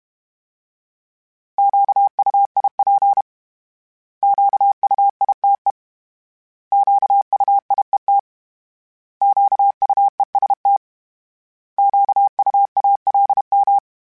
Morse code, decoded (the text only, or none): QUIP QUITE QUIET QUEST QUALM